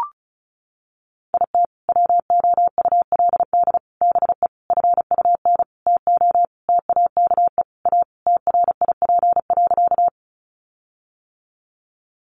Morse code, heard as a keyed tone, about 34 wpm, 700 Hz. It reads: IT WOULD BE FUN TO TAKE A TRIP.
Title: Sentences from Top 1300 Words 35wpm